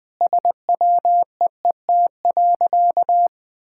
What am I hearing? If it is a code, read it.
Morse code, heard as a keyed tone, about 20 wpm, 700 Hz. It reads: SWEET.
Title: Sentences with 3-Word Phrases - 20wpm